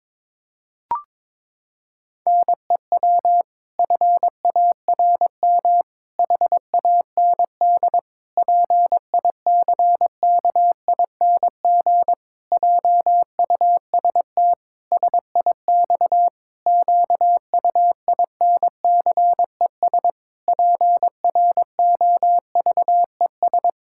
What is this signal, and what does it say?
Morse code, heard as a keyed tone, about 22 wpm, 700 Hz. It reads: NEW FARM HAND PICKING JUST SIX QUINCES PROVES
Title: Pangram Challenge v2 - 22wpm